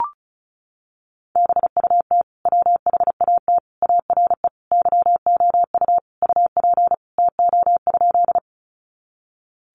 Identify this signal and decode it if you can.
Morse code, heard as a keyed tone, about 34 wpm, 700 Hz. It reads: BUT WHAT ARE YOU UP TO?
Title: Sentences from Top 100 Words 35wpm